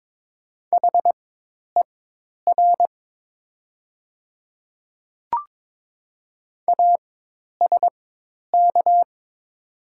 Morse code, heard as a keyed tone, about 23 wpm, spaced as for 10 wpm, 700 Hz. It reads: HER ASK
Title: 4x Character Spacing - Top 200 Words - 22wpm